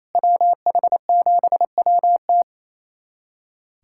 Morse code, heard as a keyed tone, about 27 wpm, 700 Hz. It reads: WH7WT
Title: USA Callsign DE Callsign - 5 Characters - 28wpm